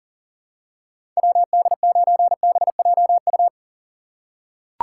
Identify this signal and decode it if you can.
Morse code, heard as a keyed tone, about 37 wpm, 700 Hz. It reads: WD9BJU
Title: USA Call Signs - 6 Characters - 3x Character Spacing - 40wpm